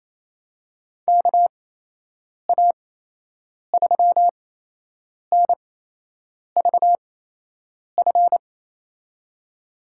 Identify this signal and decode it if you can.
Morse code, heard as a keyed tone, about 27 wpm, spaced as for 8 wpm, 700 Hz. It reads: KA3NVF
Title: USA Call Signs - 6 Characters - 8x Character Spacing - 28wpm